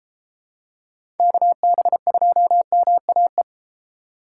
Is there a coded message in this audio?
Yes: Morse code from 1.2 to 3.4 s, heard as a keyed tone, about 32 wpm, 700 Hz. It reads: KB2MAE